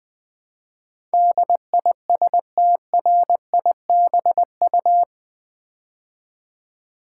Morse code, heard as a keyed tone, about 20 wpm, 700 Hz. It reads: DISTRIBU
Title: Build A Word - 4 Letters At A Time - 12-Letter Words - 20wpm